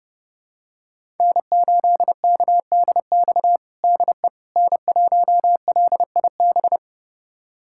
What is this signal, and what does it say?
Morse code, heard as a keyed tone, about 30 wpm, 700 Hz. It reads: N8KDX DE N1LIB
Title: USA Callsign DE Callsign - 5 Characters - 30wpm